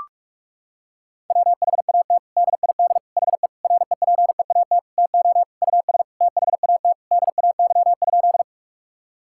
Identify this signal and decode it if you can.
Morse code, heard as a keyed tone, about 46 wpm, 700 Hz. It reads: WHAT DID HE REPEAT TO US THAT DAY?
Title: Sentences from Top 1600 Words 45wpm